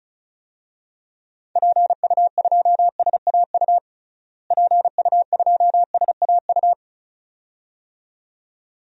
Morse code, heard as a keyed tone, about 36 wpm, 700 Hz. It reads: PU2SAU PU2SAU
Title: DXCC-100 Call Signs - 6 Characters - Repeated 2x - 35wpm